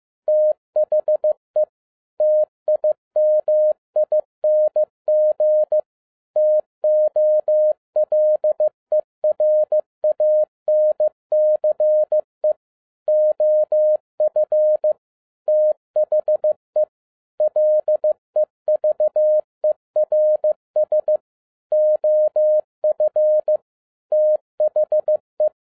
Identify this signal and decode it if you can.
Morse code, heard as a keyed tone, about 15 wpm, 610 Hz. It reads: THE TIMING TOLERANCE OF THE LEVERS OF THE